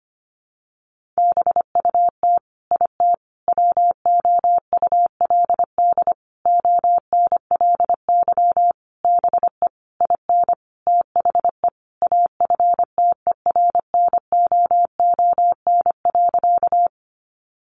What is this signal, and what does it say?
Morse code, heard as a keyed tone, about 25 wpm, 700 Hz. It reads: BUT IT WOULD ONLY BE IN THE AFTERNOON.
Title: Sentences from Top 1500 Words 25wpm